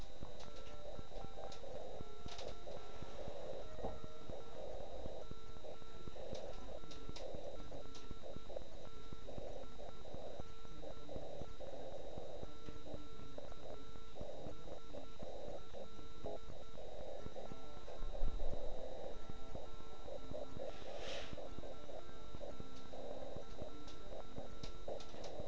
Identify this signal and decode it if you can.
Morse code, heard as a keyed tone, about 9 wpm, 1340 Hz, with noise. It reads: BK R MARTINE QSLL 73 <AR>